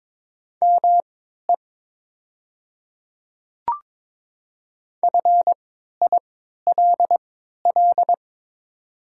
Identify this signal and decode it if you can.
Morse code, heard as a keyed tone, about 22 wpm, spaced as for 12 wpm, 700 Hz. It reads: ME FILL
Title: 3x Character Spacing - Top 500 Words - 22wpm